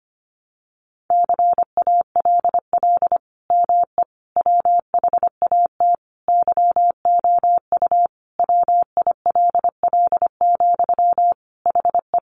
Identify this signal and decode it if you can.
Morse code, heard as a keyed tone, about 25 wpm, 700 Hz. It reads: CALL ME WHAT YOU WILL, HE
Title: Sentences from Top 200 Words 25wpm